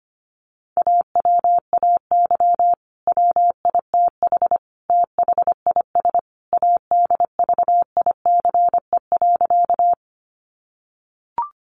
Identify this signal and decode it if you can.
Morse code, heard as a keyed tone, about 25 wpm, 700 Hz. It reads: AWAY WITH THIS ADVICE.
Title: Sentences from Top 1100 Words 25wpm